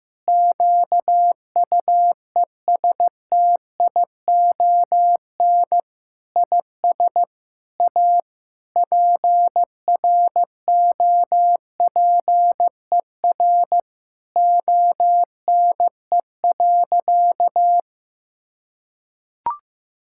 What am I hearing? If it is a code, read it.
Morse code, heard as a keyed tone, about 15 wpm, 700 Hz. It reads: QUESTION IS A PROPER ONE.
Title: Sentences from Top 1000 Words - Review Part 5 - 15wpm